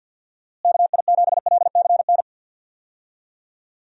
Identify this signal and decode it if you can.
Morse code, heard as a keyed tone, about 51 wpm, 700 Hz. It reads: KI7LCN
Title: USA Call Signs - 6 Characters - Repeated 2x - 50wpm